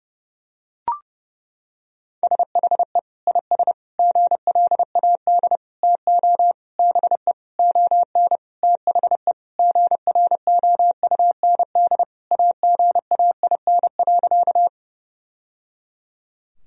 Morse code, heard as a keyed tone, about 32 wpm, 700 Hz. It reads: SHE IS GLAD TO BE ON THE GROUND AGAIN.
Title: Sentences from Top 1000 Words - Review Part 2 - 30wpm